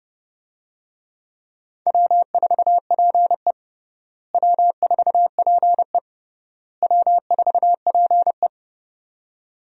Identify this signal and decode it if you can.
Morse code, heard as a keyed tone, about 30 wpm, 700 Hz. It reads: W4PE W4PE W4PE
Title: USA Call Signs - 4 Characters - Repeated 3x - 30wpm